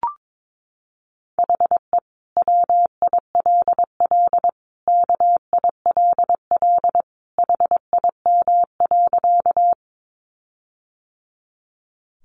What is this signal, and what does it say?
Morse code, heard as a keyed tone, about 22 wpm, 700 Hz. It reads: HE WILL KILL HIM.
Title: Sentences from Top 1400 Words 22wpm